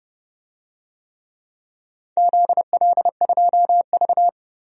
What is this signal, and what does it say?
Morse code, heard as a keyed tone, about 29 wpm, 700 Hz. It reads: ZL2V
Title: DXCC-100 Call Signs - 4 Characters - 30wpm